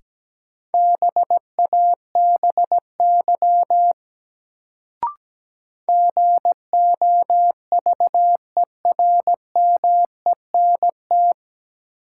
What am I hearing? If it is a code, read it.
Morse code, heard as a keyed tone, about 17 wpm, 700 Hz. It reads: BABY GOVERMENT
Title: Sentences from Top 1300 Words - New Word Drill - 17wpm